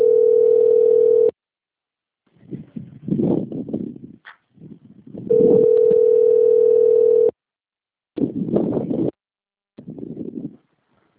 Ringback tone (North America), 440 + 480 Hz.
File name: rg-1600-unknown- - - .44066